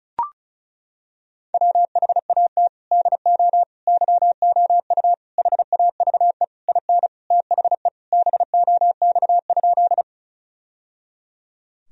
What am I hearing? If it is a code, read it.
Morse code, heard as a keyed tone, about 36 wpm, 700 Hz. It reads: WHAT DO YOU HAVE IN THE BOX?
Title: Sentences from Top 1000 Words - Review Part 2 - 35wpm